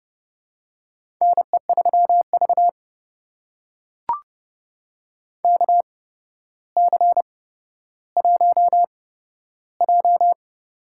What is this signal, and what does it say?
Morse code, heard as a keyed tone, about 30 wpm, spaced as for 13 wpm, 700 Hz. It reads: NE3V KC1J
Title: USA Call Signs - 4 Characters - 8x Character Spacing - 30wpm